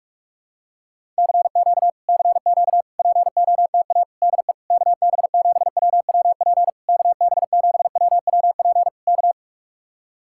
Morse code, heard as a keyed tone, about 49 wpm, 700 Hz. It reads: CQ CQ POTA DE KB7WWP KB7WWP K